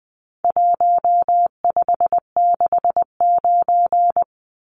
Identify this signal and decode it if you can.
Morse code, heard as a keyed tone, about 20 wpm, 700 Hz. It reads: 1569
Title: Numbers - Contest Style - 4-Digit 20wpm